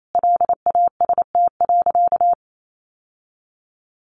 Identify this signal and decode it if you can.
Morse code, heard as a keyed tone, about 28 wpm, 700 Hz. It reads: LAST.
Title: Sentences from Top 1500 Words 28wpm